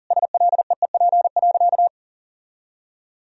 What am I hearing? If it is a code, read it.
Morse code, heard as a keyed tone, about 40 wpm, 700 Hz. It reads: SLEEP.
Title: Sentences from Top 1000 Words - Review Part 2 - 40wpm